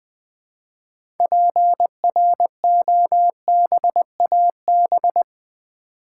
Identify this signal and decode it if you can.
Morse code, heard as a keyed tone, about 20 wpm, 700 Hz. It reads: PROBAB
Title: Build A Word - 2 Letters At A Time - 8-Letter Words - 20wpm